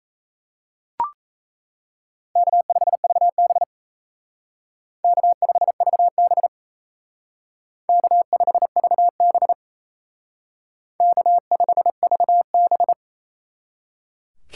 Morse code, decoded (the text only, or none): K5VB K5VB K5VB K5VB